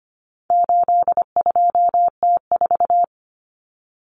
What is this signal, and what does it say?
Morse code, heard as a keyed tone, about 25 wpm, 700 Hz. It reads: 82T4
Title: Common Cut Numbers - 4-Digit 25wpm